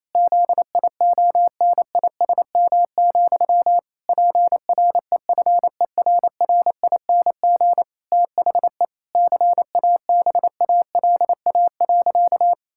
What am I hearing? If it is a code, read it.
Morse code, heard as a keyed tone, about 28 wpm, 700 Hz. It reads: ZIONISM, PREFERRING THE CABALA.
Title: Pangram Challenge v2 - 28wpm